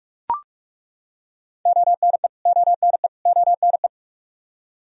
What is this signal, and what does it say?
Morse code, heard as a keyed tone, about 43 wpm, 700 Hz. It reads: ONE ONE ONE
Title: Top 100 Words Repeated 45wpm